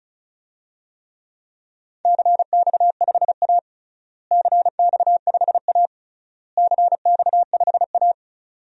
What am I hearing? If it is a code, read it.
Morse code, heard as a keyed tone, about 36 wpm, 700 Hz. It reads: CX5A CX5A CX5A
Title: DXCC-100 Call Signs - 4 Characters - Repeated 3x - 35wpm